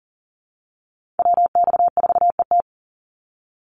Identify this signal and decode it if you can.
Morse code, heard as a keyed tone, about 40 wpm, 700 Hz. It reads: WX4ET
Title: NA SOTA Chaser Callsigns 40wpm